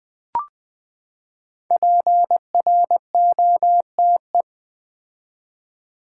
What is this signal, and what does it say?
Morse code, heard as a keyed tone, about 20 wpm, 700 Hz. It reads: PROTE
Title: Build A Word - 5 Letters At A Time - 10-Letter Words - 20wpm